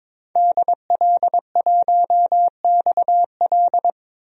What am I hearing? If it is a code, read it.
Morse code, heard as a keyed tone, about 22 wpm, 700 Hz. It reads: DL1XL
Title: DXCC-100 Call Signs - 5 Characters - 22wpm